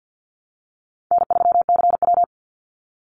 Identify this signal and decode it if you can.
Morse code, heard as a keyed tone, about 48 wpm, 700 Hz. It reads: N3CW
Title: Easy Call Signs Repeated 3x - 50wpm